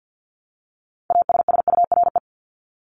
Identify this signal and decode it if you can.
Morse code, heard as a keyed tone, about 50 wpm, 700 Hz. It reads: ASSURE
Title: Top 1501-1750 Words 50wpm